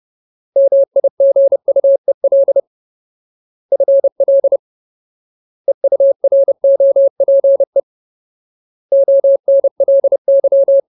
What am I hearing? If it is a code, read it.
Morse code, heard as a keyed tone, about 30 wpm, 550 Hz. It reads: MIGUEL FL EUROPE ONLY
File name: Ultimate Word List / 30WPM Ultimate